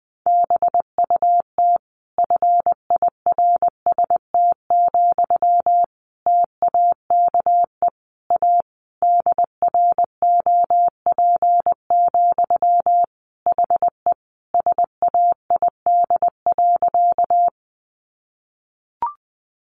Morse code, heard as a keyed tone, about 19 wpm, 700 Hz. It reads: BUT FIRST, TAKE A DROP, HE SAID.
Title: Sentences from Top 1400 Words 20wpm